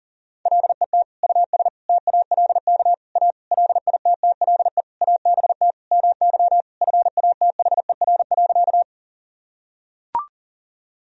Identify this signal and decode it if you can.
Morse code, heard as a keyed tone, about 40 wpm, 700 Hz. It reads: LET US TALK A LITTLE ABT MY FATHER.
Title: Sentences from Top 400 Words and Abbreviations 40wpm